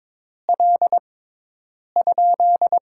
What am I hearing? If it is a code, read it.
Morse code, heard as a keyed tone, about 22 wpm, spaced as for 8 wpm, 700 Hz. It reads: L?
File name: Top 90 QSO Elements - 6x Character Spacing - 22wpm